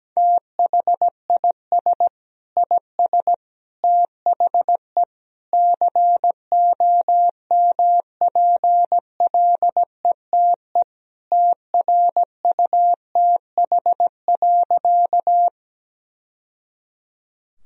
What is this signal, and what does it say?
Morse code, heard as a keyed tone, about 17 wpm, 700 Hz. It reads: THIS IS THE COMPLETE TRUTH.
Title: Sentences from Top 1000 Words - Review Part 2 - 17wpm